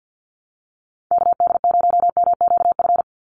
Morse code, heard as a keyed tone, about 50 wpm, 700 Hz. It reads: KD0MQF
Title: USA Call Signs 50wpm